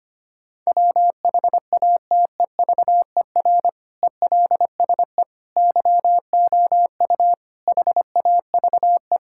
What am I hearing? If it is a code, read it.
Morse code, heard as a keyed tone, about 25 wpm, 700 Hz. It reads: WHATEVER ELSE YOU HAVE
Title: Sentences from Top 1600 Words 25wpm